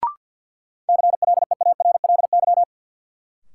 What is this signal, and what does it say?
Morse code, heard as a keyed tone, about 48 wpm, 700 Hz. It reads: CLEARLY
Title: Sentences from Top 1300 Words - New Word Drill Alt - 50wpm